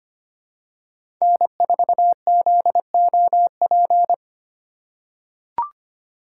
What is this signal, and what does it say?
Morse code, heard as a keyed tone, about 25 wpm, 700 Hz. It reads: N4ZOP
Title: USA Call Signs - 2x Character Spacing - 25wpm